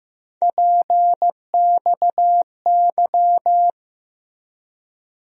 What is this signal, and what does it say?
Morse code, heard as a keyed tone, about 15 wpm, 700 Hz. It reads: PXY
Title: USA Call Sign Postfixes - 3 Letters 15wpm